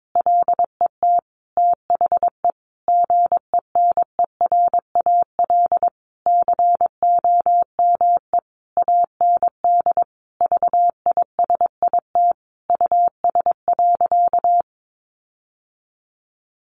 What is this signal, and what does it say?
Morse code, heard as a keyed tone, about 22 wpm, 700 Hz. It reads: LET THE GENERAL COME AND VISIT US.
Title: Sentences from Top 1400 Words 22wpm